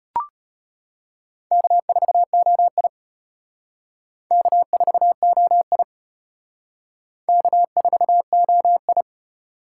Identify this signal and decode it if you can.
Morse code, heard as a keyed tone, about 32 wpm, 700 Hz. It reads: K4OI K4OI K4OI